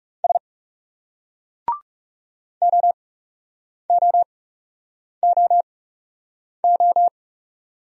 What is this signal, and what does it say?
Morse code, heard as a keyed tone, about 38 wpm, spaced as for 8 wpm, 700 Hz. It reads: S OOOO